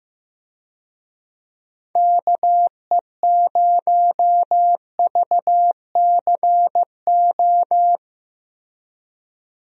Morse code, heard as a keyed tone, about 15 wpm, 700 Hz. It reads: KE0VCO
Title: USA Call Signs - 6 Characters - Repeated 2x - 15wpm